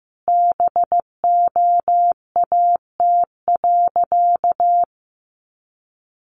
Morse code, heard as a keyed tone, about 15 wpm, 700 Hz. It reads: BOAT.
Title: Sentences from Top 1500 Words 15wpm